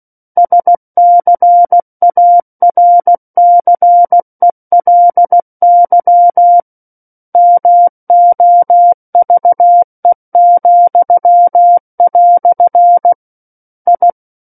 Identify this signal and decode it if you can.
Morse code, heard as a keyed tone, about 16 wpm, 700 Hz. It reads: SCARCELY MOVE," I
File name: Book One / 07-Chapter 7 - 16 WPM